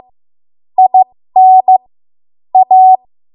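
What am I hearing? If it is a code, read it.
Morse code, heard as a keyed tone, about 15 wpm, spaced as for 13 wpm, 750 Hz. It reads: IN A
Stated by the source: 13 wpm Morse code practice archive transmission.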